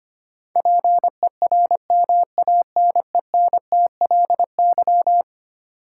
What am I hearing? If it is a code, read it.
Morse code, heard as a keyed tone, about 25 wpm, 700 Hz. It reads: PERMANENTLY
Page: morse-code-ninja-practice-25wpm